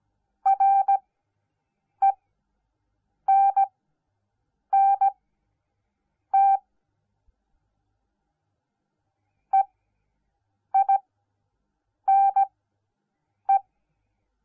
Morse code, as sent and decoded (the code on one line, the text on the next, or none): .-. -. -. - / . .. -. .
RENNT EINE